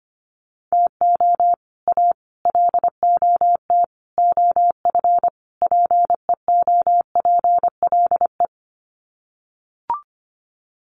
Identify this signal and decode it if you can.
Morse code, heard as a keyed tone, about 25 wpm, 700 Hz. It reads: TO A LOT OF PEOPLE
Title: Sets of 5 Words Encore 25wpm